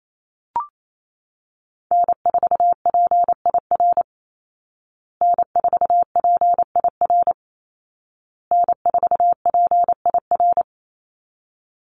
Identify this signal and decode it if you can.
Morse code, heard as a keyed tone, about 28 wpm, 700 Hz. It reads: N4PIR N4PIR N4PIR
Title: Call Signs Repeated 2x - 28wpm